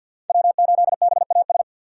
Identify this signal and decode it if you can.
Morse code, heard as a keyed tone, about 52 wpm, 700 Hz. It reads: W8BAS
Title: USA Call Signs - 5 Characters - 2x Character Spacing - 50wpm